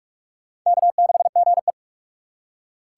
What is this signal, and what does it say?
Morse code, heard as a keyed tone, about 45 wpm, 700 Hz. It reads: K6GE
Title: USA Call Signs - 4 Characters - 6x Character Spacing - 45wpm